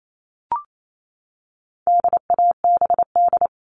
Morse code, heard as a keyed tone, about 28 wpm, 700 Hz. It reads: DABD
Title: All Cut Numbers - 4-Digit 28wpm